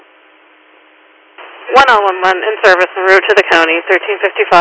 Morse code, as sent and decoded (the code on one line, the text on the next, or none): none